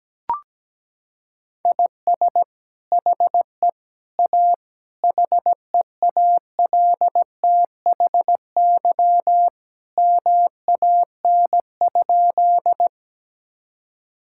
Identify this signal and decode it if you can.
Morse code, heard as a keyed tone, about 17 wpm, 700 Hz. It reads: IS HE A HEALTHY MAN?
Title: Sentences from Top 1000 Words - Review Part 5 - 17wpm